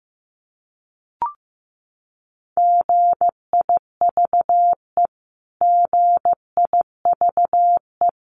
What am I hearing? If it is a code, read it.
Morse code, heard as a keyed tone, about 15 wpm, 700 Hz. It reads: GIVE GIVE
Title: Top 500 Words Repeated 15wpm